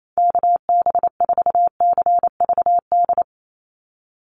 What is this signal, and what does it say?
Morse code, heard as a keyed tone, about 28 wpm, 700 Hz. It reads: KB4CVD
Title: USA Call Signs 28wpm